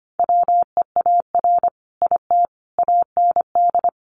Morse code, heard as a keyed tone, about 25 wpm, 700 Hz. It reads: WEAR IT AND
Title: Sentences from Top 1500 Words 25wpm